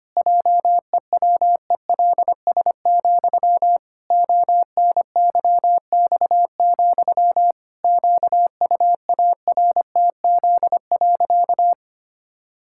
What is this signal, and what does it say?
Morse code, heard as a keyed tone, about 25 wpm, 700 Hz. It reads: JEWELS, ONYX, QUARTZ.
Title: Pangram Challenge v2 - 25wpm